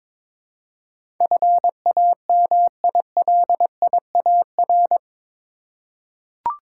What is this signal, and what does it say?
Morse code, heard as a keyed tone, about 21 wpm, 700 Hz. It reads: FAMILIAR
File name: practice / 22wpm / Eight Letter Words - 22wpm